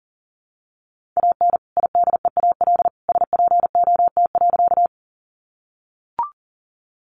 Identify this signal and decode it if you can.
Morse code, heard as a keyed tone, about 40 wpm, 700 Hz. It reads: AN IDEAL SPOT.
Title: Sentences from Top 1400 Words 40wpm